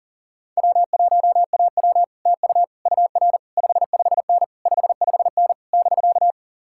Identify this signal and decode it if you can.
Morse code, heard as a keyed tone, about 37 wpm, 700 Hz. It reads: W1AW TU UR 55N 55N <BK>
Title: Speed-Racing Common QSO Phrases Encore - 30wpm